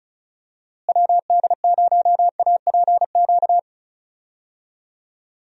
Morse code, heard as a keyed tone, about 36 wpm, 700 Hz. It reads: WD0APQ
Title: USA Call Signs - 6 Characters - Repeated 3x - 35wpm